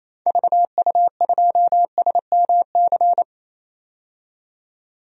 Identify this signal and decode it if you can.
Morse code, heard as a keyed tone, about 28 wpm, 700 Hz. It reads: VU2SMC
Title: DXCC-100 Call Signs - 6 Characters - 28wpm